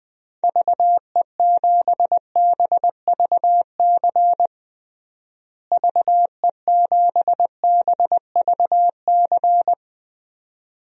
Morse code, heard as a keyed tone, about 21 wpm, 700 Hz. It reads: VE7BVC VE7BVC
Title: DXCC-100 Call Signs - 6 Characters - Repeated 2x - 20wpm